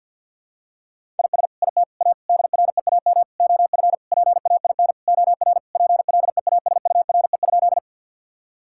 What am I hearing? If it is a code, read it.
Morse code, heard as a keyed tone, about 50 wpm, 700 Hz. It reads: IS IT A DREAM OF PAIN OR PLEASURE?
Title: Sentences from Top 1600 Words 50wpm